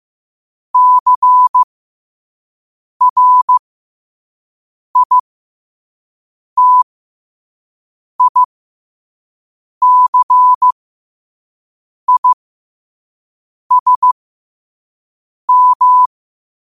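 Morse code, heard as a keyed tone, about 15 wpm, spaced as for 5 wpm, 1000 Hz. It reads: CRITICISM
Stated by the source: Morse code practice at 5 wpm.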